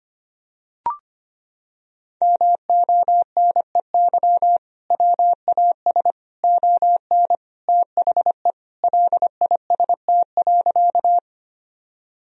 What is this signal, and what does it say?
Morse code, heard as a keyed tone, about 25 wpm, 700 Hz. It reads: MONEY WAS ON THE LIST.
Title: Sentences from Top 1000 Words - Review Part 2 - 25wpm